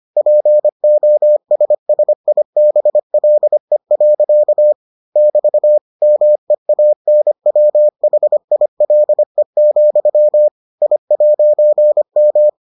Morse code, heard as a keyed tone, about 25 wpm, 600 Hz. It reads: POSSIBLE. <BT> MEANWHILE, I'M